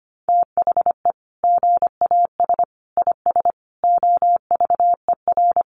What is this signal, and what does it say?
Morse code, heard as a keyed tone, about 25 wpm, 690 Hz. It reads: THE GAS IS OVER
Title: Sentences from Top 1100 Words 25wpm